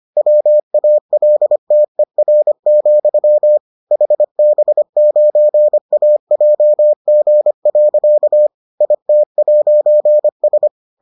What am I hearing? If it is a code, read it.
Morse code, heard as a keyed tone, about 25 wpm, 600 Hz. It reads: WALTER, HB9AJG. IT'S